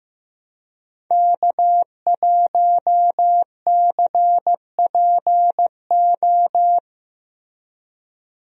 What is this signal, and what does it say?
Morse code, heard as a keyed tone, about 16 wpm, 700 Hz. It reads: K1CPO
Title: USA Call Signs - 5 Characters - Repeated 2x - 15wpm